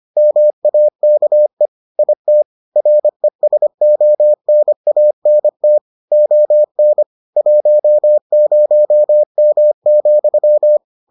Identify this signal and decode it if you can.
Morse code, heard as a keyed tone, about 25 wpm, 600 Hz. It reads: MAKE IT RESONANT ON 10M,